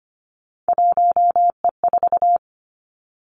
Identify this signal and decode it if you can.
Morse code, heard as a keyed tone, about 25 wpm, 700 Hz. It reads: 1E4
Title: Three Challenging Letter-Number Combinations 25wpm